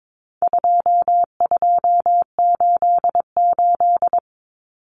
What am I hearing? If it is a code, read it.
Morse code, heard as a keyed tone, about 22 wpm, 700 Hz. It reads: 2288
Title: Common Cut Numbers - 4-Digit 22wpm